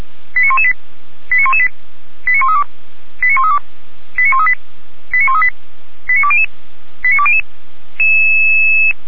Five-tone selective call (ZVEI), heard as quick runs of five tones, tones of about 70 ms. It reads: 89118 89118 89132 89132 89138 89138 89200 89200
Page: SDR/ZVEI